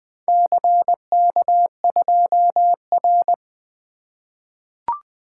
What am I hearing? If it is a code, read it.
Morse code, heard as a keyed tone, about 20 wpm, 700 Hz. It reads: CK2R
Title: Four Letter-Number Combinations - 20wpm